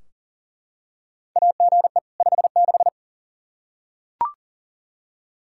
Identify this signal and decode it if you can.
Morse code, heard as a keyed tone, about 40 wpm, 700 Hz. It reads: AGE 56
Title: K1USN SST QSO Practice 40wpm